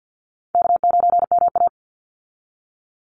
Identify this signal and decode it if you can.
Morse code, heard as a keyed tone, about 48 wpm, 700 Hz. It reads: K9MA
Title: Call Signs Repeated 3x - 50wpm